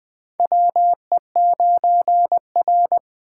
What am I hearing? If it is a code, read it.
Morse code, heard as a keyed tone, about 20 wpm, 700 Hz. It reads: WE9R
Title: USA Call Signs - 4 Characters - 6x Character Spacing - 20wpm